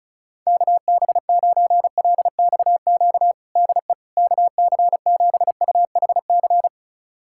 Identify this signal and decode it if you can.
Morse code, heard as a keyed tone, about 36 wpm, 700 Hz. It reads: KB9LXQ DE KC7UHC
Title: USA Callsign DE Callsign - 6 Characters - 35wpm